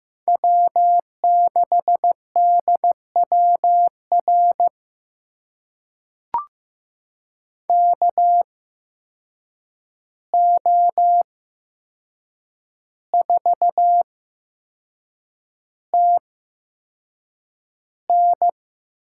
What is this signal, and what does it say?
Morse code, heard as a keyed tone, about 16 wpm, spaced as for 6 wpm, 700 Hz. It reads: W6DWR KO4TN
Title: USA Call Signs - 5 Characters - 8x Character Spacing - 15wpm